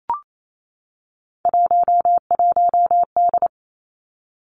Morse code, heard as a keyed tone, about 28 wpm, 700 Hz. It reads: 11D
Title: Three Challenging Letter-Number Combinations 28wpm